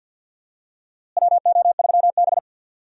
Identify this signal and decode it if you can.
Morse code, heard as a keyed tone, about 47 wpm, 700 Hz. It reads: WO3B
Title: USA Call Signs - 4 Characters - Repeated 3x - 50wpm